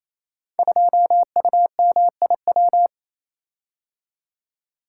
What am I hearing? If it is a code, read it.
Morse code, heard as a keyed tone, about 29 wpm, 700 Hz. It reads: 2UMIW
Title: Code Groups - 5 Letter:Numbers - 28wpm